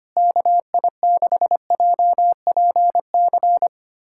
Morse code, heard as a keyed tone, about 25 wpm, 700 Hz. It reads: KI6JPC
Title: USA Callsign DE Callsign 25wpm